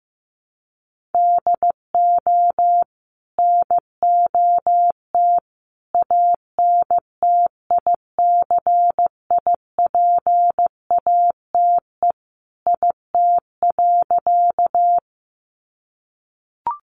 Morse code, heard as a keyed tone, about 15 wpm, 700 Hz. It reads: DO NOT ANTICIPATE IT.
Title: Sentences from Top 1500 Words 15wpm